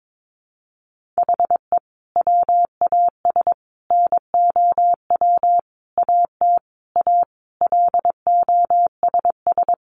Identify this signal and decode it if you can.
Morse code, heard as a keyed tone, about 22 wpm, 700 Hz. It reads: HE WAS NOW AT A LOSS
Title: Sentences from Top 1400 Words 22wpm